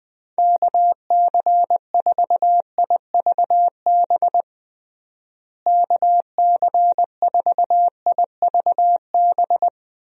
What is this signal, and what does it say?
Morse code, heard as a keyed tone, about 21 wpm, 700 Hz. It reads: KC4IVB KC4IVB
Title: USA Call Signs - 6 Characters - Repeated 3x - 20wpm